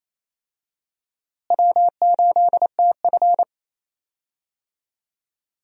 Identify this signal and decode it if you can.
Morse code, heard as a keyed tone, about 28 wpm, 700 Hz. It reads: W8TF
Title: USA Call Signs - 4 Characters - Repeated 3x - 28wpm